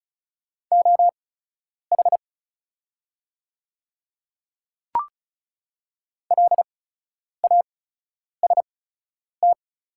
Morse code, heard as a keyed tone, about 36 wpm, spaced as for 10 wpm, 700 Hz. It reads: OH LAST
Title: 8x Character Spacing - Top 500 Words - 35wpm